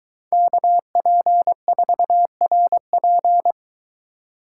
Morse code, heard as a keyed tone, about 23 wpm, 700 Hz. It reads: KP4RP